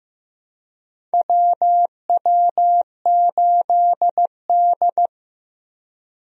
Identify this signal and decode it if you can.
Morse code, heard as a keyed tone, about 16 wpm, 700 Hz. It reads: WW8D
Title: USA Call Signs - 4 Characters - Repeated 2x - 15wpm